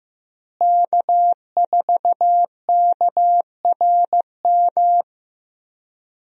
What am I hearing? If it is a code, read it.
Morse code, heard as a keyed tone, about 15 wpm, 700 Hz. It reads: K4KRM
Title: USA Call Signs - 5 Characters - Repeated 3x - 15wpm